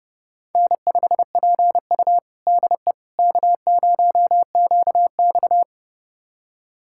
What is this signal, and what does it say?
Morse code, heard as a keyed tone, about 30 wpm, 700 Hz. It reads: N5PU DE K0QX
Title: USA Callsign DE Callsign - 4 Characters - 30wpm